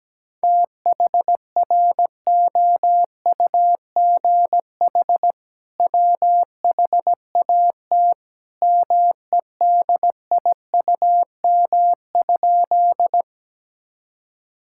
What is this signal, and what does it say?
Morse code, heard as a keyed tone, about 17 wpm, 700 Hz. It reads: THROUGH WHAT MEDIUM?
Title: Sentences from Top 1000 Words - Review Part 6 - 17wpm